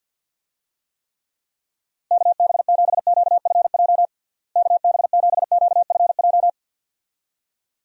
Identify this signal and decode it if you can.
Morse code, heard as a keyed tone, about 52 wpm, 700 Hz. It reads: KB7QFJ KB7QFJ
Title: USA Call Signs - 6 Characters - Repeated 2x - 50wpm